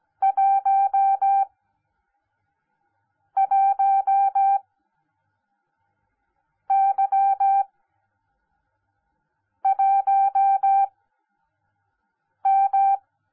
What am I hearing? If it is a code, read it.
Morse code, heard as a keyed tone, about 17 wpm, spaced as for 4 wpm, 780 Hz. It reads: 11Y1M
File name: Onlinelehrgang / morsen / DJ4UF Morsen10